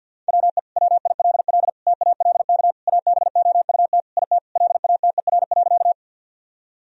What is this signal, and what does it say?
Morse code, heard as a keyed tone, about 50 wpm, 700 Hz. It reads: WE WILL TALK ABOUT IT LATER.
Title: Sentences from Top 1000 Words - Review Part 4 - 50wpm